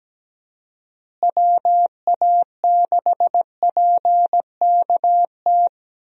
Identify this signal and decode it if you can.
Morse code, heard as a keyed tone, about 17 wpm, 700 Hz. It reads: WA6PKT
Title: USA Call Signs - 6 Characters - Repeated 2x - 17wpm